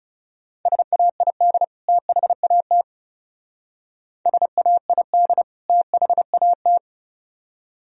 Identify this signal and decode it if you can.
Morse code, heard as a keyed tone, about 39 wpm, 700 Hz. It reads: SAID THAT SAID THAT